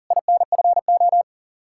Morse code, heard as a keyed tone, about 40 wpm, 700 Hz. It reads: INFO
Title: Practice QSO - Exchange 3 - More Info - 40wpm